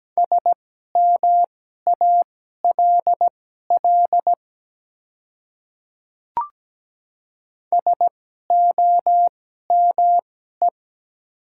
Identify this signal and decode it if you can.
Morse code, heard as a keyed tone, about 17 wpm, spaced as for 12 wpm, 700 Hz. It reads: SMALL SOME
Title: 2x Character Spacing - Top 300 Words - 17wpm